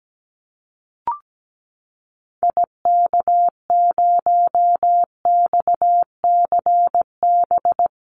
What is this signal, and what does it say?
Morse code, heard as a keyed tone, about 17 wpm, 700 Hz. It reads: IK0XCB
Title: Call Signs - 17wpm